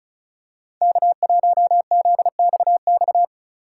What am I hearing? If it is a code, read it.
Morse code, heard as a keyed tone, about 34 wpm, 700 Hz. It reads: K1ZXX
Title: USA Call Signs - 5 Characters - 3x Character Spacing - 35wpm